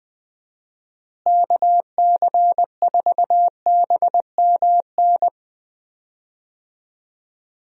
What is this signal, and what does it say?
Morse code, heard as a keyed tone, about 20 wpm, 700 Hz. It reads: KC4BMN
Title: USA Call Signs - 6 Characters - Repeated 3x - 20wpm